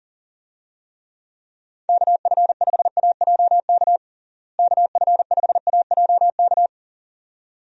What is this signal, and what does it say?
Morse code, heard as a keyed tone, about 37 wpm, 700 Hz. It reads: KF5AJK KF5AJK
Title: USA Call Signs - 6 Characters - Repeated 2x - 40wpm